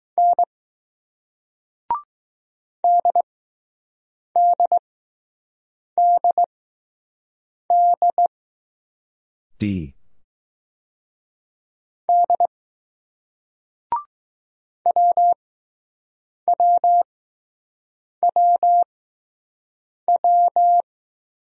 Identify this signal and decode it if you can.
Morse code, heard as a keyed tone, about 20 wpm, spaced as for 6 wpm, 700 Hz. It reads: N DDDD D WWWW